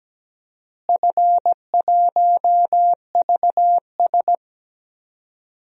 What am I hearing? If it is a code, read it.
Morse code, heard as a keyed tone, about 17 wpm, 700 Hz. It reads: F1VS
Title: DXCC-100 Call Signs - 4 Characters - Repeated 2x - 17wpm